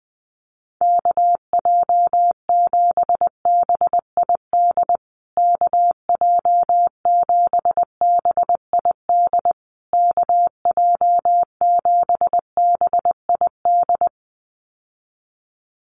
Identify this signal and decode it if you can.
Morse code, heard as a keyed tone, about 19 wpm, 700 Hz. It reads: KJ7BID KJ7BID KJ7BID
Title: USA Call Signs Repeated 3x 20wpm